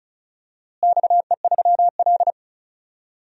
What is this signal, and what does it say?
Morse code, heard as a keyed tone, about 36 wpm, 700 Hz. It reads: XE3L
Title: DXCC-100 Call Signs - 4 Characters - Repeated 3x - 35wpm